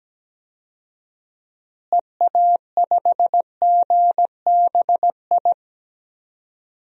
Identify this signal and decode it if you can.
Morse code, heard as a keyed tone, about 17 wpm, 700 Hz. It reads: EA5GBI
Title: DXCC-100 Call Signs - 6 Characters - Repeated 3x - 17wpm